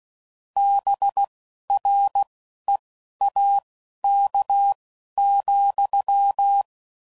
Morse code, heard as a keyed tone, about 16 wpm, spaced as for 11 wpm, 800 Hz. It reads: BREAK,
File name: CW Code Groups @10WPM / QSOs / QSO WA7QOX